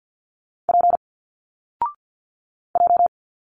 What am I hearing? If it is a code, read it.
Morse code, heard as a keyed tone, about 50 wpm, 700 Hz. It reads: P J
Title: Single Letters 50wpm